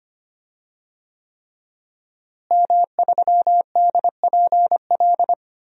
Morse code, heard as a keyed tone, about 25 wpm, 700 Hz. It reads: M3DPL
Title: DXCC-100 Call Signs - 5 Characters - Repeated 3x - 25wpm